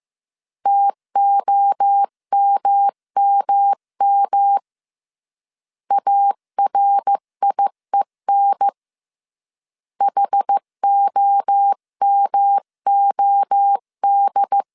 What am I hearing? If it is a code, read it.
Morse code, heard as a keyed tone, about 15 wpm, 800 Hz. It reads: TOMMM ARIEN HOMOD
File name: CW / DET / 1 3 1500C S A 14wpm 2447